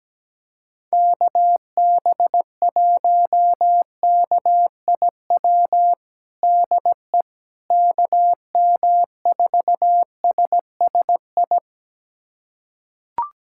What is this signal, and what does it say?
Morse code, heard as a keyed tone, about 17 wpm, 700 Hz. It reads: KB1KIW DE KM4SSI
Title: USA Callsign DE Callsign 17wpm